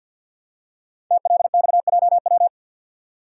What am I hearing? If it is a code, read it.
Morse code, heard as a keyed tone, about 50 wpm, 700 Hz. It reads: TLXJW
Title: Code Groups - 5 Letters - 50wpm